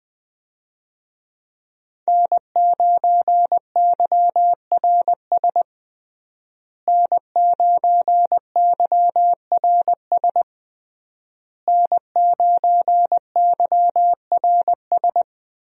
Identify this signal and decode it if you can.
Morse code, heard as a keyed tone, about 20 wpm, 700 Hz. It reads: N9YRS N9YRS N9YRS
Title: USA Call Signs - 5 Characters - Repeated 3x - 20wpm